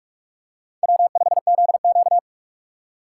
Morse code, heard as a keyed tone, about 43 wpm, 700 Hz. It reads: W5ZQ
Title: USA Call Signs - 4 Characters - 6x Character Spacing - 45wpm